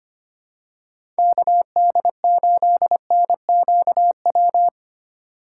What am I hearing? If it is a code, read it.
Morse code, heard as a keyed tone, about 25 wpm, 700 Hz. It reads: KD8NQW
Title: USA Call Signs - 6 Characters - 8x Character Spacing - 25wpm